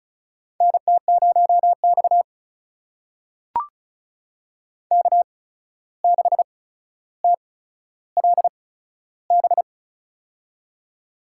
Morse code, heard as a keyed tone, about 36 wpm, spaced as for 14 wpm, 700 Hz. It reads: NT0X K6TLB
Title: USA Call Signs - 8x Character Spacing - 35wpm